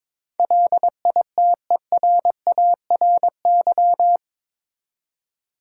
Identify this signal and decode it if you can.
Morse code, heard as a keyed tone, about 22 wpm, 700 Hz. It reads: LITERARY
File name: practice / 22wpm / Eight Letter Words - 22wpm